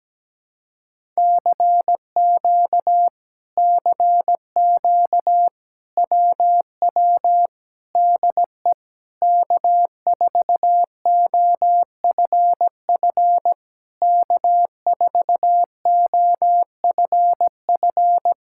Morse code, heard as a keyed tone, about 17 wpm, 700 Hz. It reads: CQ CQ WW DE K4OFF K4OFF
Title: Speed-Racing Common QSO Phrases Encore - 17wpm